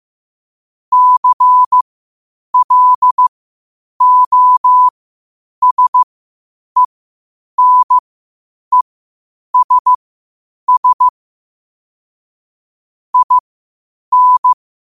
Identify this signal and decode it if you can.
Morse code, heard as a keyed tone, about 15 wpm, spaced as for 8 wpm, 1000 Hz. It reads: CLOSENESS IN
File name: qotdfiles / 8wpm